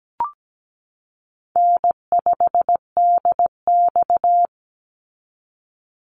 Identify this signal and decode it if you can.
Morse code, heard as a keyed tone, about 17 wpm, 700 Hz. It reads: N5DX
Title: Call Signs Repeated 3x - 17wpm